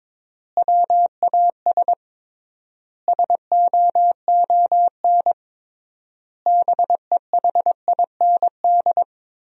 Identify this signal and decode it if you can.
Morse code, heard as a keyed tone, about 22 wpm, 700 Hz. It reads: WAS SOON BEHIND
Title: 3x Word-Spacing - Sentences from Top 500 Words - 22wpm